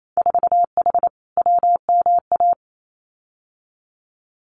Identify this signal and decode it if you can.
Morse code, heard as a keyed tone, about 28 wpm, 700 Hz. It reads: 4H WMA
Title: Winter Field Day Exchanges Encore 28wpm